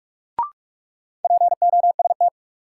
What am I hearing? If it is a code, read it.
Morse code, heard as a keyed tone, about 46 wpm, 700 Hz. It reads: POST
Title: Sentences from Top 1000 Words - New Word Drill - 45wpm